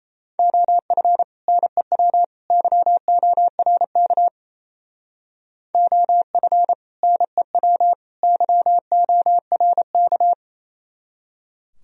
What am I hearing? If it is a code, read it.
Morse code, heard as a keyed tone, about 32 wpm, 700 Hz. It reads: OF NEW YORK OF NEW YORK
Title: Speed-Racing - Sets of 3 Words 28wpm